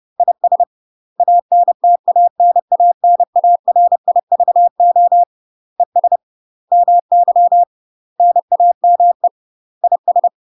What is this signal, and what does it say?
Morse code, heard as a keyed tone, about 30 wpm, 700 Hz. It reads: IS ANTANANARIVO ES MY NAME IS
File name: UK MASTER QSOs TRAINING / 30WPM / UK MASTER QSOs 0043